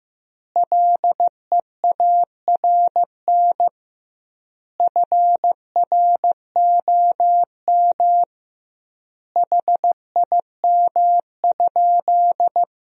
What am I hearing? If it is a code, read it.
Morse code, heard as a keyed tone, about 15 wpm, 700 Hz. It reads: LEARN FROM HIM?
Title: 2x Word-Spacing - Sentences from Top 300 Words - 15wpm